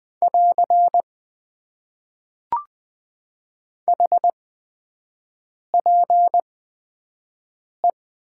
Morse code, heard as a keyed tone, about 20 wpm, spaced as for 5 wpm, 700 Hz. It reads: <AR> HPE
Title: Top 90 QSO Elements - 8x Character Spacing - 20wpm